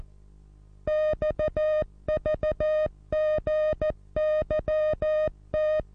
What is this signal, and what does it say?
Morse code, heard as a keyed tone, about 14 wpm, 620 Hz. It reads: XVGYT
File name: repos / Morsekurs / DJ4UF Morsen27